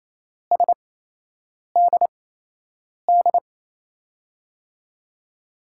Morse code, heard as a keyed tone, about 28 wpm, spaced as for 8 wpm, 700 Hz. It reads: SDD
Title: USA Call Signs - 6 Characters - 8x Character Spacing - 28wpm